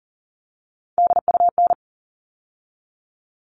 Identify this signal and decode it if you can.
Morse code, heard as a keyed tone, about 40 wpm, 700 Hz. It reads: DUN
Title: All Cut Numbers - 3-Digit 40wpm